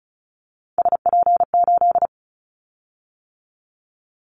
Morse code, heard as a keyed tone, about 36 wpm, 700 Hz. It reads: SP8
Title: Callsign Prefixes for Top 25 DXCC - 35wpm